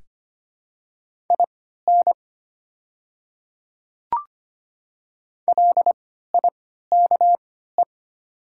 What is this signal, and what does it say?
Morse code, heard as a keyed tone, about 25 wpm, spaced as for 14 wpm, 700 Hz. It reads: IN LIKE
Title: 3x Character Spacing - Top 100 Words - 25wpm